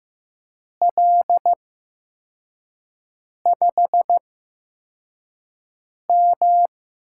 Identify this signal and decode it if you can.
Morse code, heard as a keyed tone, about 15 wpm, spaced as for 4 wpm, 700 Hz. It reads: L5M
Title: USA Call Signs - 4 Characters - 8x Character Spacing - 15wpm